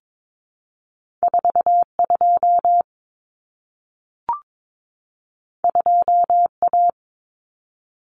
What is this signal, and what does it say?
Morse code, heard as a keyed tone, about 22 wpm, 700 Hz. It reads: 42 2A
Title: Common Cut Numbers - 2-Digit 22wpm